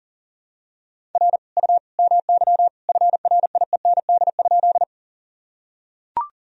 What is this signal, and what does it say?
Morse code, heard as a keyed tone, about 37 wpm, 700 Hz. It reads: R U MY FRIEND?
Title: Sentences from Top 300 Words and Abbreviations 40wpm